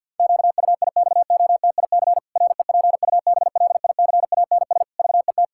Morse code, heard as a keyed tone, about 46 wpm, 700 Hz. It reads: QUIXOTIC REPUBLICANS VET
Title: Pangram Challenge v2 - 50wpm